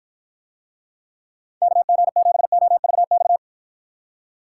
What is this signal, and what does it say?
Morse code, heard as a keyed tone, about 56 wpm, 700 Hz. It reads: KG7OVX